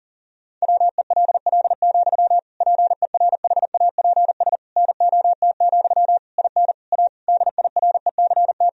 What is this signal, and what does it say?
Morse code, heard as a keyed tone, about 37 wpm, 700 Hz. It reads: WELL, PERHAPS NOT, IN A DIRECT
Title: Sentences from Top 1000 Words - Review Part 2 - 40wpm